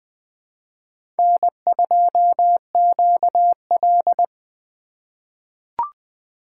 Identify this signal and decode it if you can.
Morse code, heard as a keyed tone, about 20 wpm, 700 Hz. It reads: N2QL
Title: USA Call Signs - 4 Characters - 2x Character Spacing - 20wpm